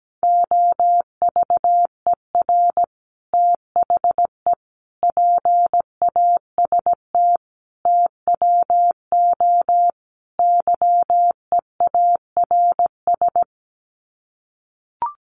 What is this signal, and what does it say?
Morse code, heard as a keyed tone, about 17 wpm, 700 Hz. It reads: OVER THE PAST TWO YEARS
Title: Sets of 5 Words Encore 17wpm